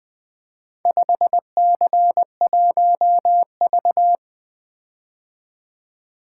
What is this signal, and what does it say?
Morse code, heard as a keyed tone, about 20 wpm, 700 Hz. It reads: 5C1V
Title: Four Letter-Number Combinations - 20wpm